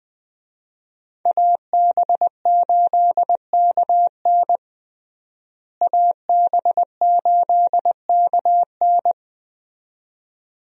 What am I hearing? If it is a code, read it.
Morse code, heard as a keyed tone, about 21 wpm, 700 Hz. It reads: AB8KN AB8KN
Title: USA Call Signs - 5 Characters - Repeated 2x - 20wpm